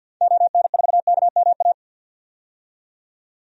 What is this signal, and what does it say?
Morse code, heard as a keyed tone, about 49 wpm, 700 Hz. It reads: ON4KMA
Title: DXCC-100 Call Signs - 6 Characters - 50wpm